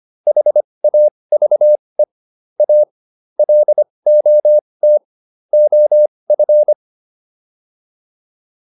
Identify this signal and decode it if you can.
Morse code, heard as a keyed tone, about 25 wpm, spaced as for 20 wpm, 600 Hz. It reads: HAVE A LOT OF